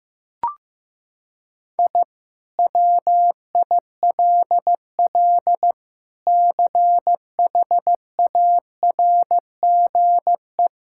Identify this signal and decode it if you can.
Morse code, heard as a keyed tone, about 16 wpm, 700 Hz. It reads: I WILL CHARGE
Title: Sentences from Top 1000 Words - Review Part 6 - 15wpm